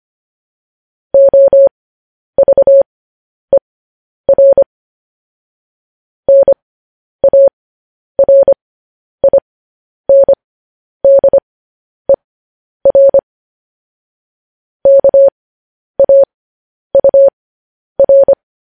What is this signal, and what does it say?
Morse code, heard as a keyed tone, about 25 wpm, spaced as for 10 wpm, 550 Hz. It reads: OVER NARINDER KAUR
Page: news-headlines-in-morse-code-at-10-wpm